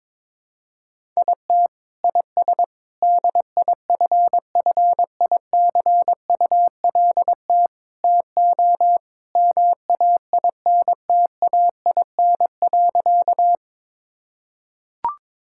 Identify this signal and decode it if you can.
Morse code, heard as a keyed tone, about 22 wpm, 700 Hz. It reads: IT IS DIFFICULT TO MAINTAIN.
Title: Sentences from Top 1000 Words - Review Part 5 - 22wpm